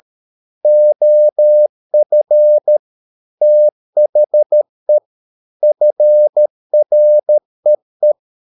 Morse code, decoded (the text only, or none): OF THE FREE